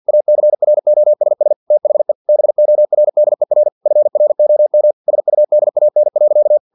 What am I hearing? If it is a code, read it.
Morse code, heard as a keyed tone, about 50 wpm, 600 Hz. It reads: ACROSS THE BORDER FROM SUDAN.
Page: Never 49wpm